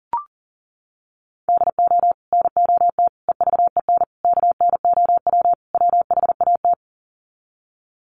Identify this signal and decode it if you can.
Morse code, heard as a keyed tone, about 40 wpm, 700 Hz. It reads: DO NOT EVEN KNOW WHAT
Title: Sets of 5 Words Encore 40wpm